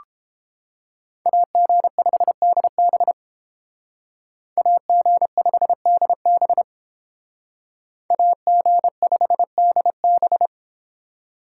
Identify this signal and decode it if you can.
Morse code, heard as a keyed tone, about 31 wpm, 700 Hz. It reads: AG5DB AG5DB AG5DB